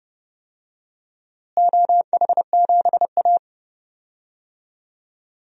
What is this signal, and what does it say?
Morse code, heard as a keyed tone, about 32 wpm, 700 Hz. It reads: OH7A
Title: DXCC-100 Call Signs - 4 Characters - Repeated 3x - 30wpm